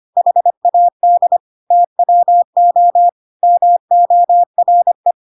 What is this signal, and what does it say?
Morse code, heard as a keyed tone, about 25 wpm, 700 Hz. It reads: HAD TWO MORE
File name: Pinocchio / Pinocchio ch2 25 25 ik1hge